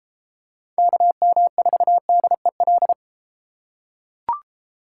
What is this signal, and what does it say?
Morse code, heard as a keyed tone, about 32 wpm, 700 Hz. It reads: KM4DEL